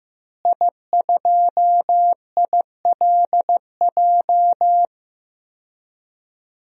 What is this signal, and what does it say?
Morse code, heard as a keyed tone, about 15 wpm, 700 Hz. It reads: I2ILJ
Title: DXCC-100 Call Signs - 5 Characters - Repeated 2x - 15wpm